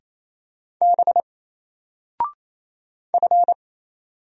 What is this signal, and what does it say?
Morse code, heard as a keyed tone, about 28 wpm, 700 Hz. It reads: B F
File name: Single-Letter-Number-28wpm